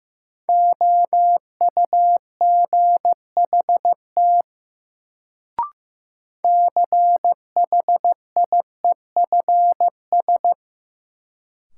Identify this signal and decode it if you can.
Morse code, heard as a keyed tone, about 15 wpm, 700 Hz. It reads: OUGHT CHIEFS
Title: Sets of 5 Words Encore - New Word Drill - 15wpm